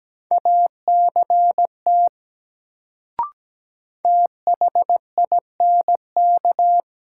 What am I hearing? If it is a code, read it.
Morse code, heard as a keyed tone, about 17 wpm, 700 Hz. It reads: ACT THINK
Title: Sentences from Top 200 Words - New Word Drill - 17wpm